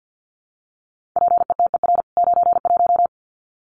Morse code, heard as a keyed tone, about 48 wpm, 700 Hz. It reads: PETER 91
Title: CWT Exchange 50wpm